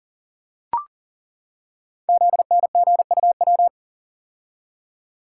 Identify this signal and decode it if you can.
Morse code, heard as a keyed tone, about 42 wpm, 690 Hz. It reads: ZNGUW
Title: Code Groups - 5 Letters - 40wpm